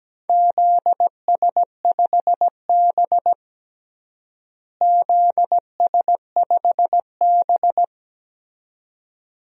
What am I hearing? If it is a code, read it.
Morse code, heard as a keyed tone, about 17 wpm, 700 Hz. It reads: ZS5B ZS5B
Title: DXCC-100 Call Signs - 4 Characters - Repeated 2x - 17wpm